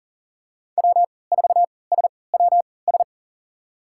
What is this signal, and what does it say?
Morse code, heard as a keyed tone, about 40 wpm, spaced as for 23 wpm, 700 Hz. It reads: W4SWS
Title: USA Call Signs - 5 Characters - 3x Character Spacing - 40wpm